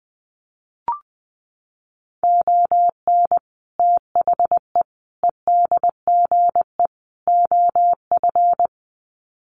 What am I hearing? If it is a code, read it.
Morse code, heard as a keyed tone, about 20 wpm, 700 Hz. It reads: ON THE EDGE OF
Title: Sets of 4 Words 20wpm